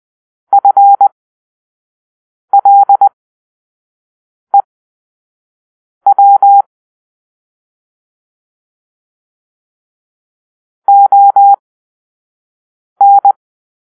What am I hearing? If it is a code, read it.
Morse code, heard as a keyed tone, about 20 wpm, spaced as for 5 wpm, 800 Hz. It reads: FLEW ON